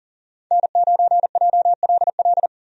Morse code, heard as a keyed tone, about 37 wpm, 700 Hz. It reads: N9JLL
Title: USA Call Signs - 5 Characters - Repeated 3x - 40wpm